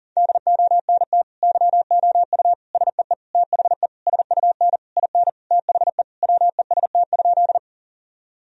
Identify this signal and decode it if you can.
Morse code, heard as a keyed tone, about 40 wpm, 700 Hz. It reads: DONT YOU SEE THE SUN IN THE WEST?
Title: Sentences from Top 1000 Words - Review Part 2 - 40wpm